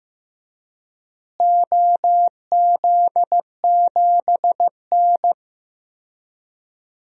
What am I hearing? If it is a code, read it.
Morse code, heard as a keyed tone, about 15 wpm, 700 Hz. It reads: OZ7N
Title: DXCC-100 Call Signs - 4 Characters - Repeated 3x - 15wpm